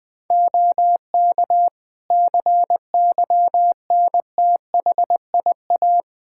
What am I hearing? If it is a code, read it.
Morse code, heard as a keyed tone, about 20 wpm, 700 Hz. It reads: OK CYNTHIA
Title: K1USN SST QSO Practice 20wpm